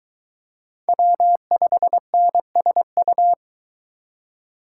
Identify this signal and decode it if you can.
Morse code, heard as a keyed tone, about 22 wpm, 700 Hz. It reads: W5NSU